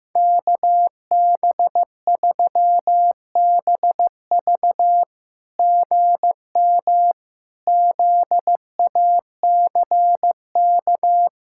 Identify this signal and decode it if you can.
Morse code, heard as a keyed tone, about 15 wpm, 700 Hz. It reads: KB3BV GM ZACK
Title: SOTA QSO Practice 15wpm